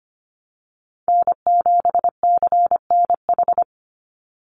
Morse code, heard as a keyed tone, about 25 wpm, 700 Hz. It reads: N7CNH
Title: NA SOTA Chaser Callsigns 25wpm